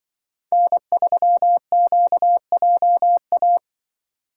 Morse code, heard as a keyed tone, about 25 wpm, 700 Hz. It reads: N3QJA
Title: Speed Racing - USA 1x3 Call Signs 20wpm